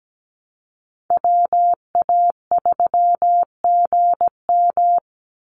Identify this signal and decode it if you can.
Morse code, heard as a keyed tone, about 17 wpm, 700 Hz. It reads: WA3GM
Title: Call Signs Repeated 2x - 17wpm